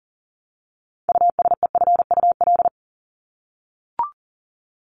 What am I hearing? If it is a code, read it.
Morse code, heard as a keyed tone, about 40 wpm, 700 Hz. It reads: USEFUL
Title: Top 501-750 Words 40wpm